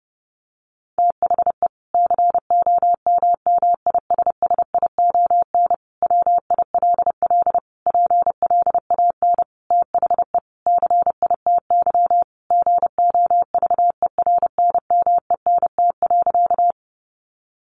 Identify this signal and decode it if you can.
Morse code, heard as a keyed tone, about 30 wpm, 700 Hz. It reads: THE COMMISSION WILL PLAN THE CITY GOVERNMENT.
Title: Sentences from Top 1400 Words 30wpm